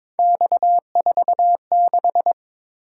Morse code, heard as a keyed tone, about 22 wpm, 700 Hz. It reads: X46
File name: practice / 22wpm / Three Letter-Number Combinations - 22wpm